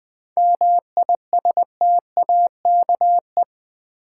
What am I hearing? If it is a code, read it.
Morse code, heard as a keyed tone, about 20 wpm, 700 Hz. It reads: MISTAKE
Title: Seven Letter Words - 20wpm